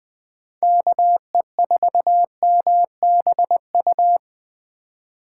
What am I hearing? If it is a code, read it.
Morse code, heard as a keyed tone, about 24 wpm, 700 Hz. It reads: KE4MBU
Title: Speed Racing - USA 2x3 Call Signs 20wpm